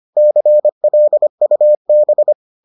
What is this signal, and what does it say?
Morse code, heard as a keyed tone, about 25 wpm, 600 Hz. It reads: CLUB